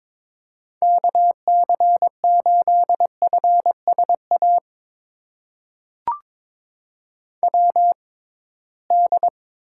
Morse code, heard as a keyed tone, about 21 wpm, spaced as for 16 wpm, 700 Hz. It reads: KC8FSA WD
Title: USA Call Signs - 6 Characters - 6x Character Spacing - 22wpm